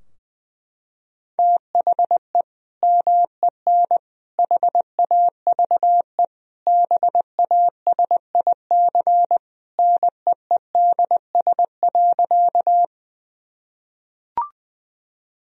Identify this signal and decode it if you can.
Morse code, heard as a keyed tone, about 20 wpm, 700 Hz. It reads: THE MEN HAVE BASIC NEEDS.
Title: Sentences from Top 1000 Words - Review Part 3 - 20wpm